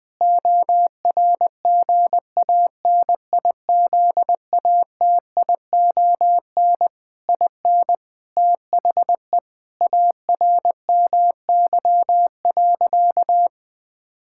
Morse code, heard as a keyed tone, about 21 wpm, 700 Hz. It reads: ORGANIZATION IN THE ARMY.
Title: Sentences from Top 1000 Words - Review Part 5 - 20wpm